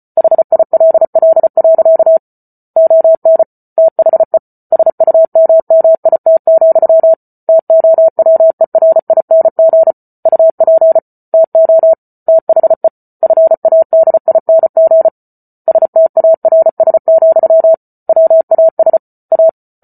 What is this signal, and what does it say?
Morse code, heard as a keyed tone, about 34 wpm, 650 Hz. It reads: HILL. ON THE SUMMIT, TOWERING UP TO THE FADING STARS, WAS A
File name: Book Two / 08-Chapter 8 - 34 WPM